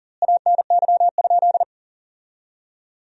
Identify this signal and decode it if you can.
Morse code, heard as a keyed tone, about 40 wpm, 700 Hz. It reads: ANY?
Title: 4x Word-Spacing - Sentences from Top 300 Words - 40wpm